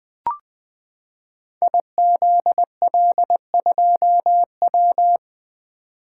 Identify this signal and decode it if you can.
Morse code, heard as a keyed tone, about 20 wpm, 700 Hz. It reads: IZL2W
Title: Code Groups - 5 Letter:Numbers - 20wpm